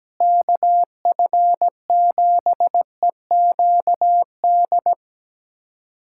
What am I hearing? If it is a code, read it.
Morse code, heard as a keyed tone, about 17 wpm, 700 Hz. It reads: KF7EQD
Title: USA Callsign DE Callsign - 6 Characters - 17wpm